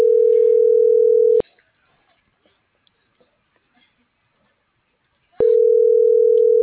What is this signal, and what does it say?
Ringback tone (North America), 440 + 480 Hz.